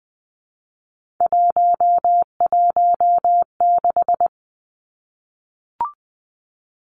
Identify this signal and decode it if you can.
Morse code, heard as a keyed tone, about 20 wpm, 700 Hz. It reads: 116
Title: Numbers - 3-Digit 20wpm